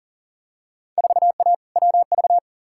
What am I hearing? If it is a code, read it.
Morse code, heard as a keyed tone, about 37 wpm, 700 Hz. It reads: 4A WV
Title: Field Day Exchanges Encore - 40wpm